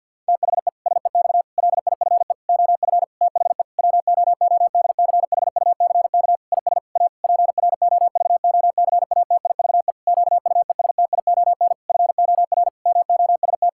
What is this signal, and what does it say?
Morse code, heard as a keyed tone, about 50 wpm, 700 Hz. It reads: THE SEX LIFE OF THE WOODCHUCK IS A PROVOCATIVE QUESTION FOR MOST
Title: Pangram Challenge v2 - 50wpm